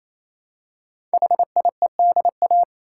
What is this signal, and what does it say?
Morse code, heard as a keyed tone, about 28 wpm, 700 Hz. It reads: HIEDA
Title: Code Groups - 5 Letters - 28wpm